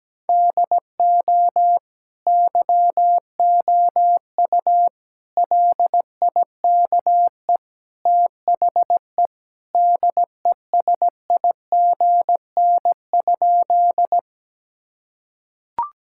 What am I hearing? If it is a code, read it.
Morse code, heard as a keyed tone, about 17 wpm, 700 Hz. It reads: DO YOU LIKE THE DESIGN?
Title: Sentences from Top 1000 Words - Review Part 5 - 17wpm